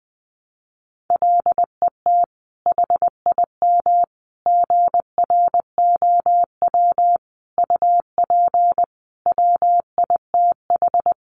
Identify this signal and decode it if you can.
Morse code, heard as a keyed tone, about 20 wpm, 700 Hz. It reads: LET HIM GROW UP WITH
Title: Sentences from Top 300 Words 20wpm